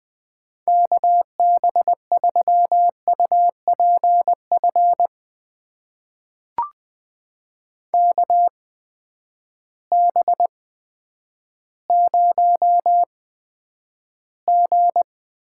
Morse code, heard as a keyed tone, about 21 wpm, spaced as for 10 wpm, 700 Hz. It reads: KB3UPF KB0G
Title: USA Call Signs - 6 Characters - 8x Character Spacing - 20wpm